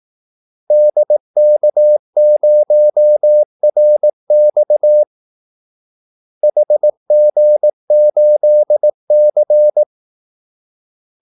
Morse code, heard as a keyed tone, about 18 wpm, 600 Hz. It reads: DK0RX HG8C
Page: Kaldesignaler 18 wpm